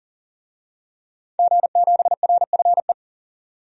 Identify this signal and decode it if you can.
Morse code, heard as a keyed tone, about 40 wpm, 700 Hz. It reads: G7RFE
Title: DXCC-100 Call Signs - 5 Characters - 40wpm